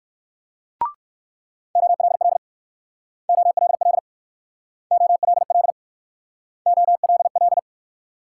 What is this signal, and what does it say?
Morse code, heard as a keyed tone, about 60 wpm, 700 Hz. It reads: OLL OLL OLL OLL